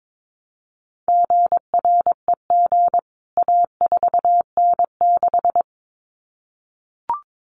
Morse code, heard as a keyed tone, about 22 wpm, 700 Hz. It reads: GREG A4N6
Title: CWT Exchange 22wpm